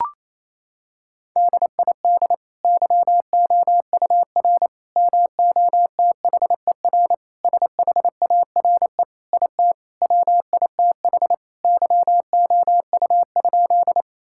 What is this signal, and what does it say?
Morse code, heard as a keyed tone, about 27 wpm, 700 Hz. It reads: DID YOUR MOTHER SHARE IT WITH YOU?
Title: Sentences from Top 1000 Words - Review Part 6 - 28wpm